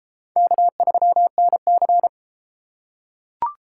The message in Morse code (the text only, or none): K3NC